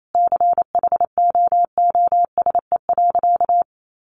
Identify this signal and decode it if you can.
Morse code, heard as a keyed tone, about 28 wpm, 700 Hz. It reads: CHOOSE.
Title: Sentences from Top 700 Words 28wpm